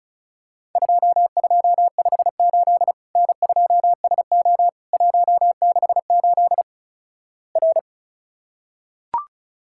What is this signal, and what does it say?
Morse code, heard as a keyed tone, about 36 wpm, 700 Hz. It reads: 2258 N2SO 168 R
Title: Worked All Europe DX Contest - QTC Exchange - 35wpm